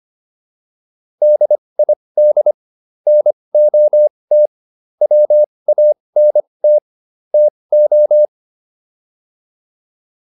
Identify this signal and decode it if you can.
Morse code, heard as a keyed tone, about 25 wpm, spaced as for 20 wpm, 600 Hz. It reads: DID NOT WANT TO